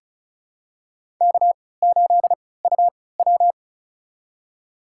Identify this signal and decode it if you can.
Morse code, heard as a keyed tone, about 34 wpm, spaced as for 20 wpm, 700 Hz. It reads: K8UW
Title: USA Call Signs - 4 Characters - 3x Character Spacing - 35wpm